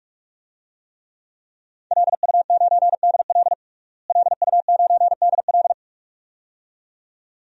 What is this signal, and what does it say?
Morse code, heard as a keyed tone, about 46 wpm, 700 Hz. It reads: LU9DL LU9DL
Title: DXCC-100 Call Signs - 5 Characters - Repeated 2x - 45wpm